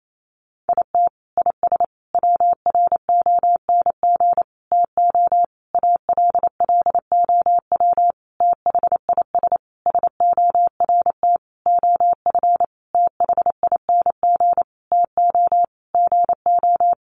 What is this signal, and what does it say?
Morse code, heard as a keyed tone, about 28 wpm, 700 Hz. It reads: IT IS WRONG TO ALLOW THIS SORT OF THING TO GO
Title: Sentences from Top 1300 Words 28wpm